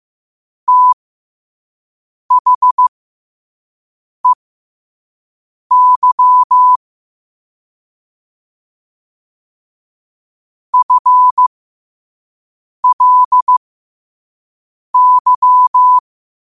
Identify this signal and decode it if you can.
Morse code, heard as a keyed tone, about 15 wpm, spaced as for 5 wpm, 1000 Hz. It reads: THEY FLY
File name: qotdfiles / 5wpm